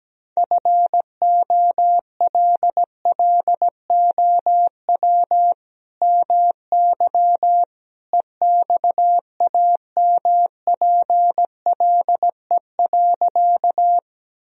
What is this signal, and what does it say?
Morse code, heard as a keyed tone, about 17 wpm, 700 Hz. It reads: FOLLOW MY EXAMPLE.
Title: Sentences from Top 1000 Words - Review Part 1 - 17wpm